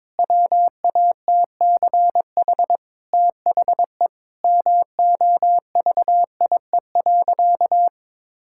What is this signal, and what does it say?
Morse code, heard as a keyed tone, about 21 wpm, 700 Hz. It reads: WATCH THE MOVIE.
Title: Sentences from Top 1000 Words - Review Part 5 - 22wpm